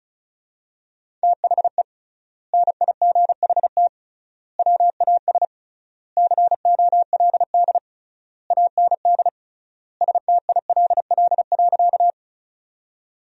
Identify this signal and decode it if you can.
Morse code, heard as a keyed tone, about 36 wpm, 700 Hz. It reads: THE NIGHT WAS COLD AND STILL.
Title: 3x Word-Spacing - Sentences from Top 400 Words - 35wpm